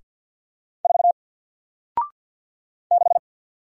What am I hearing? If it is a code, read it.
Morse code, heard as a keyed tone, about 48 wpm, 700 Hz. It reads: 4 6
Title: Single Character (Letter:Number:Punctuation:Prosign) Alt 50wpm